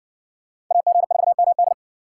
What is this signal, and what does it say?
Morse code, heard as a keyed tone, about 60 wpm, 700 Hz. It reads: AG4MD